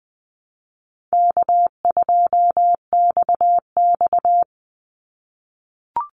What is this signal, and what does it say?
Morse code, heard as a keyed tone, about 20 wpm, 700 Hz. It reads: K2XX
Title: Easy Call Signs - 20wpm